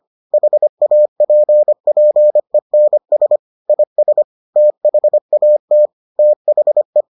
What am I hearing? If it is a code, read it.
Morse code, heard as a keyed tone, about 25 wpm, 600 Hz. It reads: HAPPENS IS THAT THE